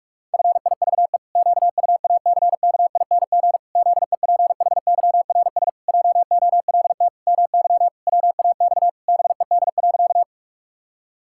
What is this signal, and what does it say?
Morse code, heard as a keyed tone, about 46 wpm, 700 Hz. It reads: FIVE QUACKING ZEPHYRS JOLT MY WAX BED.
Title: Pangram Challenge v2 - 45wpm